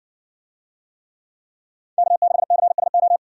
Morse code, heard as a keyed tone, about 58 wpm, 700 Hz. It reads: K6CSO